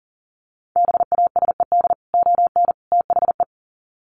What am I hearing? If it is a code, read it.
Morse code, heard as a keyed tone, about 40 wpm, 700 Hz. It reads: BASED ON THE
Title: Sets of 3 Words 40wpm